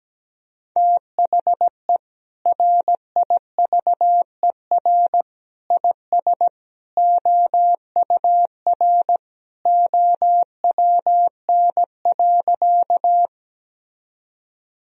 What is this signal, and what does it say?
Morse code, heard as a keyed tone, about 17 wpm, 700 Hz. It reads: THE RIVER IS OUR OWN.
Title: Sentences from Top 1000 Words - Review Part 1 - 17wpm